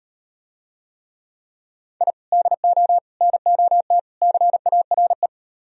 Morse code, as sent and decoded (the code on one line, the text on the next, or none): .. / -.. --- / -. --- - / -.-. .- .-. .
I DO NOT CARE